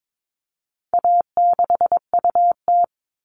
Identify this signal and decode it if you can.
Morse code, heard as a keyed tone, about 22 wpm, 700 Hz. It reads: A6UT
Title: All Cut Numbers - 4-Digit 22wpm